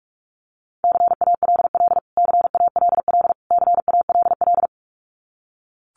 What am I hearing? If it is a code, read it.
Morse code, heard as a keyed tone, about 44 wpm, 700 Hz. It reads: CALL CALL CALL
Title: Top 300 Words Repeated 45wpm